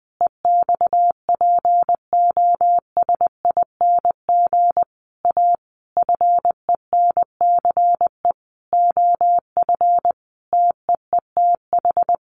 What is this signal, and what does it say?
Morse code, heard as a keyed tone, about 19 wpm, 700 Hz. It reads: EXPOSING A FENCE OF TEETH
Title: A Sound Of Thunder 20wpm